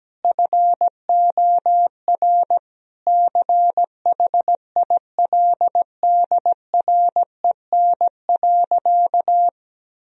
Morse code, heard as a keyed tone, about 17 wpm, 700 Hz. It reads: FOR CHILDREN.
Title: Sentences from Top 1000 Words - Review Part 4 - 17wpm